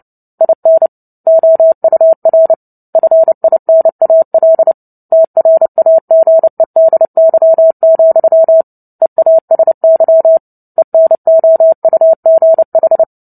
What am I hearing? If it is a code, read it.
Morse code, heard as a keyed tone, about 30 wpm, 650 Hz. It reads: IN OUR FINAL TRAGEDY, EASY ENOUGH
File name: Book Two / 03-Chapter 3 - 29 WPM